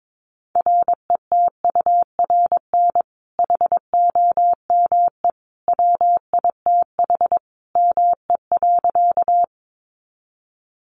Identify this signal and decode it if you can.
Morse code, heard as a keyed tone, about 22 wpm, 700 Hz. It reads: RETURN HOME WITH ME.
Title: Sentences from Top 800 Words 22wpm